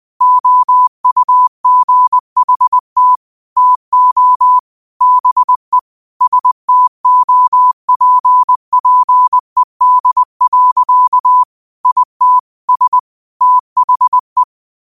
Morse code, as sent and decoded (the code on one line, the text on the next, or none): --- ..- --. .... - / - --- / -... / ... - --- .--. .--. -.. .-.-.- / .. - / ... / - .... .
OUGHT TO BE STOPPED. IT S THE